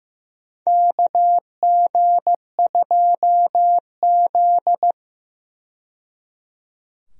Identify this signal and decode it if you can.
Morse code, heard as a keyed tone, about 15 wpm, 700 Hz. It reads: KG2Z
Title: USA Call Signs - 4 Characters - 15wpm